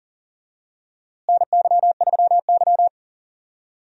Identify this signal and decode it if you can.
Morse code, heard as a keyed tone, about 40 wpm, 700 Hz. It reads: NY3Y
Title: USA Call Signs - 4 Characters - Repeated 3x - 40wpm